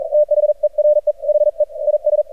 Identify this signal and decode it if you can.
Morse code, heard as a keyed tone, about 39 wpm, 600 Hz, with radio noise. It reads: THERE HE IS
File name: archived / CRSnet